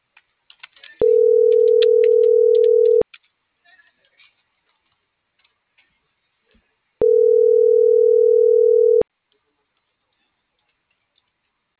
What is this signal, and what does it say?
Ringback tone (North America), 440 + 480 Hz.